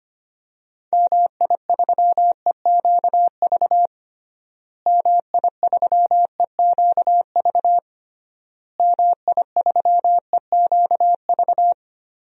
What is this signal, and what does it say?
Morse code, heard as a keyed tone, about 25 wpm, 700 Hz. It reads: MI3EQV MI3EQV MI3EQV
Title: DXCC-100 Call Signs - 6 Characters - Repeated 3x - 25wpm